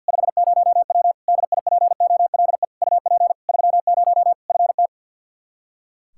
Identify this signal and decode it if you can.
Morse code, heard as a keyed tone, about 49 wpm, 700 Hz. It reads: 50W DIPOLE UP 30 FT
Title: Practice QSO - Exchange 2 - Info - 50wpm